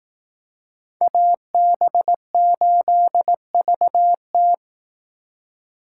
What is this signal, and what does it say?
Morse code, heard as a keyed tone, about 18 wpm, 700 Hz. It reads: AB8VT
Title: Speed Racing - USA 2x2 Call Signs 15wpm